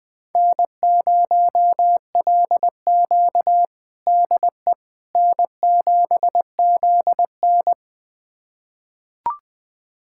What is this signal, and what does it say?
Morse code, heard as a keyed tone, about 20 wpm, 700 Hz. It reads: N0LQ DE N7ZN
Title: USA Callsign DE Callsign - 4 Characters - 20wpm